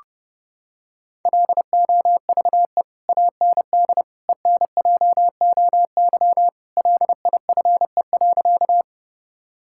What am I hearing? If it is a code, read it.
Morse code, heard as a keyed tone, about 30 wpm, 700 Hz. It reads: LOVE AND ENJOY LIFE.
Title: Sentences from Top 1000 Words - Review Part 5 - 30wpm